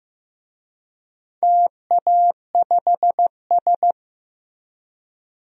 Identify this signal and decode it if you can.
Morse code, heard as a keyed tone, about 15 wpm, 700 Hz. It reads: TA5S
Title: DXCC-100 Call Signs - 4 Characters - Repeated 2x - 15wpm